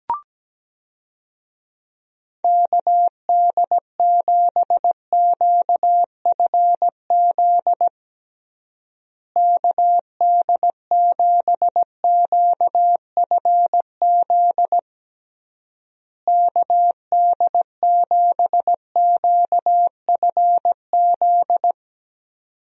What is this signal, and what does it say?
Morse code, heard as a keyed tone, about 17 wpm, 700 Hz. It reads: KD7QFZ KD7QFZ KD7QFZ
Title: USA Call Signs - 6 Characters - Repeated 3x - 17wpm